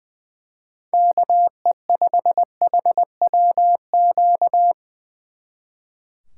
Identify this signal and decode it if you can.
Morse code, heard as a keyed tone, about 22 wpm, 700 Hz. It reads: KE5HWQ
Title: Speed Racing - USA 2x3 Call Signs 20wpm